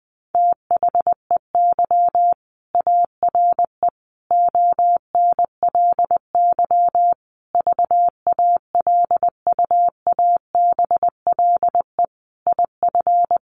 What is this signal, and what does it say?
Morse code, heard as a keyed tone, about 19 wpm, 700 Hz. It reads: THEY ARE ONLY VALUABLE IF
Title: Sentences from Top 1400 Words 20wpm